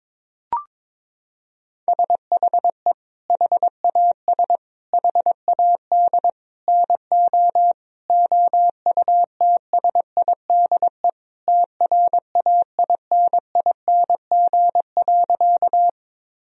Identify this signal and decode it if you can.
Morse code, heard as a keyed tone, about 22 wpm, 700 Hz. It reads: SHE HAS HAD NO OUTSIDE TRAINING.
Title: Sentences from Top 1000 Words - Review Part 4 - 22wpm